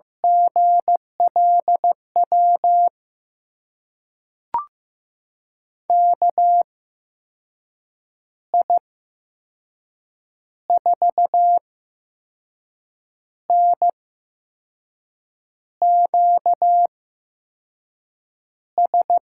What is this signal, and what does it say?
Morse code, heard as a keyed tone, about 15 wpm, spaced as for 5 wpm, 700 Hz. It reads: GLW KI4NQS
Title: USA Call Signs - 6 Characters - 8x Character Spacing - 15wpm